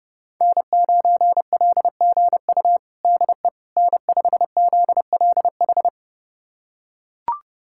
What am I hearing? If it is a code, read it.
Morse code, heard as a keyed tone, about 32 wpm, 700 Hz. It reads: N9LGU DE N5ZLH
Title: USA Callsign DE Callsign - 5 Characters - 30wpm